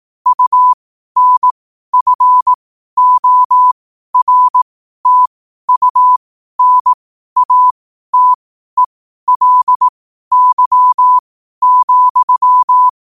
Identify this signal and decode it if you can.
Morse code, heard as a keyed tone, about 18 wpm, spaced as for 12 wpm, 1000 Hz. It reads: UNFORTUNATELY,
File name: qotdfiles / 12wpm